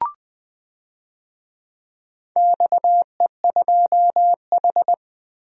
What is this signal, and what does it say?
Morse code, heard as a keyed tone, about 19 wpm, 700 Hz. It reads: XE2H
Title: DXCC-100 Call Signs - 4 Characters - Repeated 2x - 20wpm